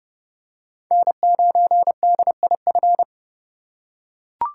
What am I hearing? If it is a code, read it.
Morse code, heard as a keyed tone, about 30 wpm, 700 Hz. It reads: N9DIF
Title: USA Call Signs - 5 Characters - 6x Character Spacing - 30wpm